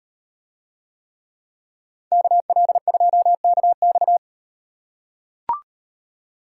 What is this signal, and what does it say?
Morse code, heard as a keyed tone, about 36 wpm, 700 Hz. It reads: KL2KX